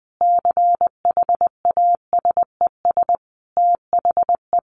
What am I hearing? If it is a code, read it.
Morse code, heard as a keyed tone, about 20 wpm, 700 Hz. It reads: CHASES THE
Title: Easy Phrases 20wpm